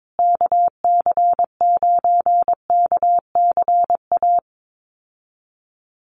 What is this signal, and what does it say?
Morse code, heard as a keyed tone, about 22 wpm, 700 Hz. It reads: KC9KCA
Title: USA Call Signs Repeated 3x 22wpm